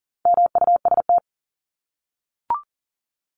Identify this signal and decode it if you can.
Morse code, heard as a keyed tone, about 40 wpm, 700 Hz. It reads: MUST
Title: Top 500 Words 40wpm